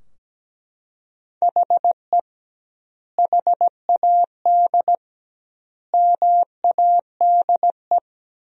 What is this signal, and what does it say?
Morse code, heard as a keyed tone, about 17 wpm, 700 Hz. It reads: HE HAD MADE
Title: 2x Word-Spacing - Sentences from Top 500 Words - 17wpm